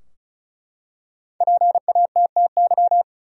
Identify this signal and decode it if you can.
Morse code, heard as a keyed tone, about 34 wpm, 700 Hz. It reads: PATTY
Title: USA Common Names - 35wpm